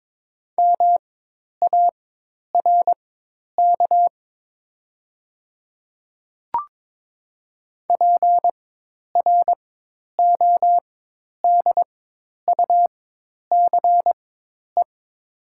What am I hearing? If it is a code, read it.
Morse code, heard as a keyed tone, about 22 wpm, spaced as for 10 wpm, 700 Hz. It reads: MARK PRODUCE
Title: 4x Character Spacing - Top 500 Words - 22wpm